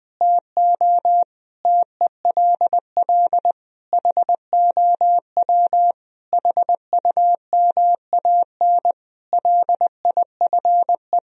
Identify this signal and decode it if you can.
Morse code, heard as a keyed tone, about 20 wpm, 700 Hz. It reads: TO TELL HOW HUMAN LIFE
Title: Sentences from Top 1000 Words - Review Part 3 - 20wpm